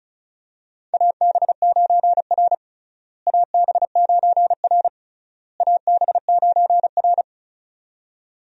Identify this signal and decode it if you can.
Morse code, heard as a keyed tone, about 36 wpm, 700 Hz. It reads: AB9R AB9R AB9R
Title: USA Call Signs - 4 Characters - Repeated 3x - 35wpm